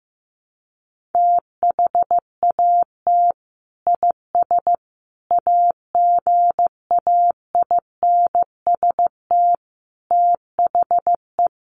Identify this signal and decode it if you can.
Morse code, heard as a keyed tone, about 15 wpm, 700 Hz. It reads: THAT IS AGAINST THE
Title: Sentences from Top 1500 Words 15wpm